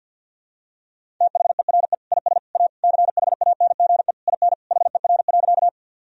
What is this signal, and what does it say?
Morse code, heard as a keyed tone, about 51 wpm, 700 Hz. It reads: THERE IS A CHANGE IN HER.
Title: Sentences from Top 1000 Words - Review Part 1 - 50wpm